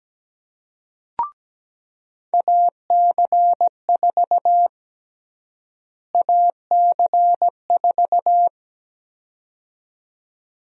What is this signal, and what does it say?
Morse code, heard as a keyed tone, about 17 wpm, 700 Hz. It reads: AC4 AC4
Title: USA Call Sign Prefixes - Repeated 2x - 17wpm